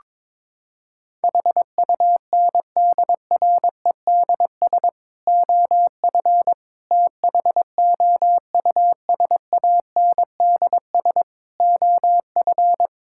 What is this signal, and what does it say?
Morse code, heard as a keyed tone, about 22 wpm, 700 Hz. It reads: HUNDREDS OF THOUSANDS OF
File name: practice / 22wpm / Sentences with 4-Word Phrases - 22wpm